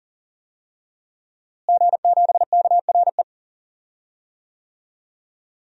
Morse code, heard as a keyed tone, about 38 wpm, 700 Hz. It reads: G7KRE
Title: DXCC-100 Call Signs - 5 Characters - Repeated 3x - 40wpm